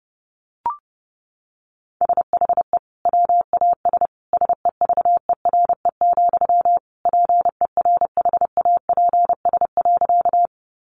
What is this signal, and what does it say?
Morse code, heard as a keyed tone, about 30 wpm, 700 Hz. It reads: SHE WAS SEVERE, PERHAPS.
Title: Sentences from Top 1400 Words 30wpm